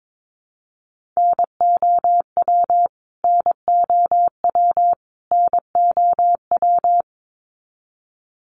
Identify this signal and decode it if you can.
Morse code, heard as a keyed tone, about 22 wpm, 700 Hz. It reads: NOW NOW NOW
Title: Top 300 Words Repeated 22wpm